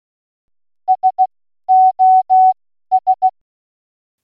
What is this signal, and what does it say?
Morse code, heard as a keyed tone, about 16 wpm, 750 Hz. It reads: S O S